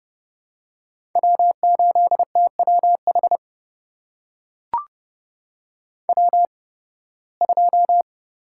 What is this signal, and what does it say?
Morse code, heard as a keyed tone, about 30 wpm, spaced as for 20 wpm, 700 Hz. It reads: W8TWH W2
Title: USA Call Signs - 8x Character Spacing - 30wpm